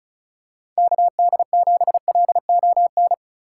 Morse code, heard as a keyed tone, about 34 wpm, 700 Hz. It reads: KD7LON
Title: USA Call Signs - 4x Character Spacing - 35wpm